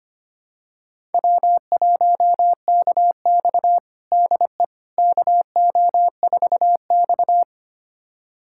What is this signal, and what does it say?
Morse code, heard as a keyed tone, about 25 wpm, 700 Hz. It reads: W1KX DE KO4X
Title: USA Callsign DE Callsign - 4 Characters - 25wpm